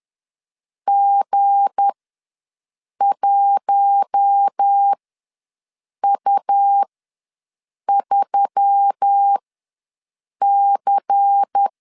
Morse code, heard as a keyed tone, about 10 wpm, spaced as for 6 wpm, 800 Hz. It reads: G1U3C
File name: LEZIONI / 7 GF9Z / 7 G ws 10wpm 3843